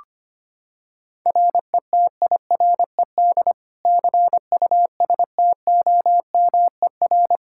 Morse code, heard as a keyed tone, about 25 wpm, 700 Hz. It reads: RETIRED CUSTOMER
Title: K1USN SST QSO Practice 25wpm